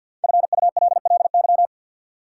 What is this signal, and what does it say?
Morse code, heard as a keyed tone, about 53 wpm, 700 Hz. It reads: FULLY
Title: Sentences from Top 1300 Words - New Word Drill Alt - 50wpm